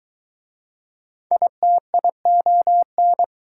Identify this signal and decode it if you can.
Morse code, heard as a keyed tone, about 23 wpm, 700 Hz. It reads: ITION